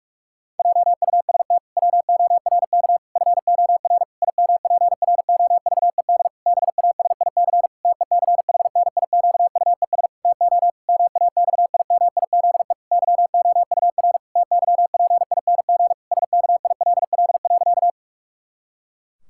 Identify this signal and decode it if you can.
Morse code, heard as a keyed tone, about 46 wpm, 700 Hz. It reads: JUST WORK FOR IMPROVED BASIC TECHNIQUES TO MAXIMIZE YOUR TYPING SKILL.
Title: Pangram Challenge v2 - 45wpm